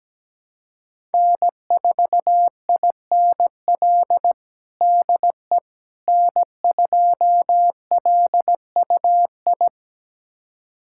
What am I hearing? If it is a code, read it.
Morse code, heard as a keyed tone, about 17 wpm, 700 Hz. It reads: N4INL DE N2LUI
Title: USA Callsign DE Callsign - 5 Characters - 17wpm